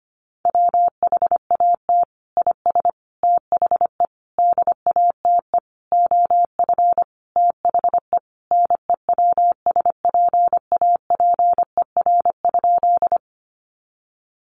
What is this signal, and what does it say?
Morse code, heard as a keyed tone, about 25 wpm, 700 Hz. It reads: WHAT IS THE DATE OF THE NEWSPAPER?
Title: Sentences from Top 1100 Words 25wpm